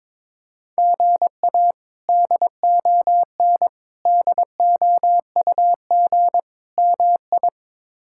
Morse code, heard as a keyed tone, about 21 wpm, 700 Hz. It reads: GA DON DOUG MI
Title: K1USN SST QSO Practice - 22wpm